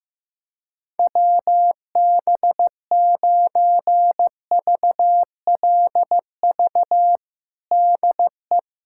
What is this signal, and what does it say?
Morse code, heard as a keyed tone, about 15 wpm, 700 Hz. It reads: WB9VLV DE
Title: USA Callsign DE Callsign 15wpm